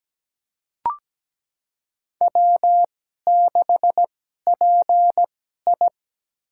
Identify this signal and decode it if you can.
Morse code, heard as a keyed tone, about 17 wpm, spaced as for 12 wpm, 700 Hz. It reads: W6PI
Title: USA Call Signs - 4 Characters - 2x Character Spacing - 17wpm